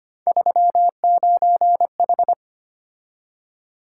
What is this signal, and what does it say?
Morse code, heard as a keyed tone, about 25 wpm, 700 Hz. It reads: 39H
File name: Three Letter-Number Combinations - 25wpm